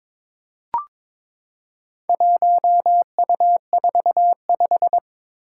Morse code, heard as a keyed tone, about 22 wpm, 700 Hz. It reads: 1U45
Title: Four Challenging Letter-Number Combinations 22wpm